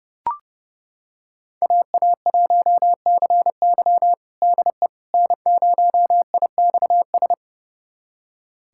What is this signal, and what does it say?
Morse code, heard as a keyed tone, about 31 wpm, 700 Hz. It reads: AA1CY DE N0IXS
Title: USA Callsign DE Callsign - 5 Characters - 30wpm